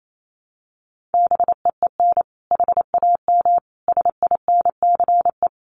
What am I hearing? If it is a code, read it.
Morse code, heard as a keyed tone, about 28 wpm, 700 Hz. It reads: BEEN HAM SINCE
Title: Common QSO Phrases 28wpm